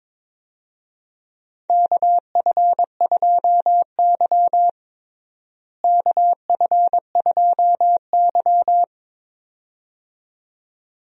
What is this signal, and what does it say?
Morse code, heard as a keyed tone, about 21 wpm, 700 Hz. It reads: KF2Y KF2Y
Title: USA Call Signs - 4 Characters - Repeated 2x - 22wpm